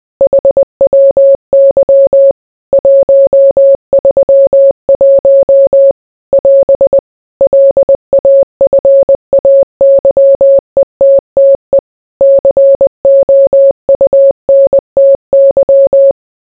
Morse code, heard as a keyed tone, about 20 wpm, 550 Hz. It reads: HWY 131 <AS> LAFAYETTE COUNTY